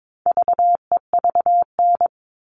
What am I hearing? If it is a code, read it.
Morse code, heard as a keyed tone, about 22 wpm, 700 Hz. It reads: VEVN
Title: All Cut Numbers - 4-Digit 22wpm